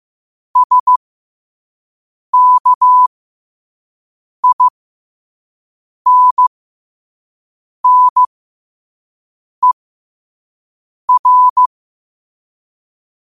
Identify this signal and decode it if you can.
Morse code, heard as a keyed tone, about 15 wpm, spaced as for 5 wpm, 1000 Hz. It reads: SKINNER
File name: qotdfiles / 5wpm